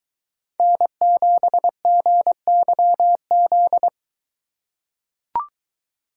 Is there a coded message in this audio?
Yes: Morse code from 0.6 to 3.9 s, heard as a keyed tone, about 22 wpm, 700 Hz. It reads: N7GYZ